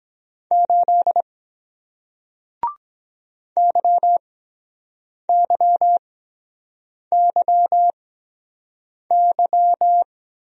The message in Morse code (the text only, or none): 8 YYYY